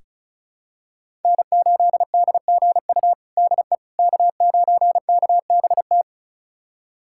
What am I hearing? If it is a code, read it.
Morse code, heard as a keyed tone, about 36 wpm, 700 Hz. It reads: N8DGU DE K9KBT
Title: USA Callsign DE Callsign - 5 Characters - 35wpm